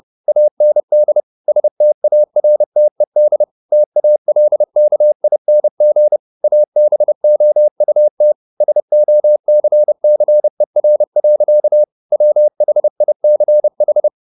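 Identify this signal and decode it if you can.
Morse code, heard as a keyed tone, about 30 wpm, 600 Hz. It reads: AND STARTED TALKING ABOUT SOCCER. WHICH
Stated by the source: Morse code at 30 wpm, the novel Never by Ken Follett in English, from CW med Gnister.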